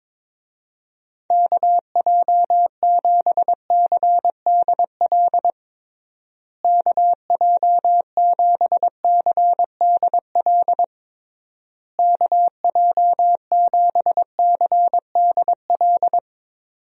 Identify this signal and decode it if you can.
Morse code, heard as a keyed tone, about 21 wpm, 700 Hz. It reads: KJ7CDL KJ7CDL KJ7CDL
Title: USA Call Signs - 6 Characters - Repeated 3x - 22wpm